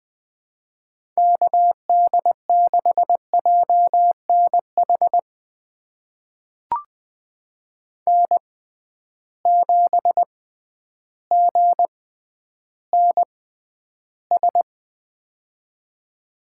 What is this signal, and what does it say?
Morse code, heard as a keyed tone, about 21 wpm, spaced as for 11 wpm, 700 Hz. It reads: KD6JNH N7GNS
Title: USA Call Signs - 6x Character Spacing - 20wpm